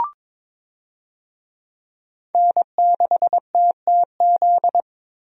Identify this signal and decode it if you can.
Morse code, heard as a keyed tone, about 22 wpm, 700 Hz. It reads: N6TTZ
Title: USA Call Signs - 5 Characters - Repeated 3x - 22wpm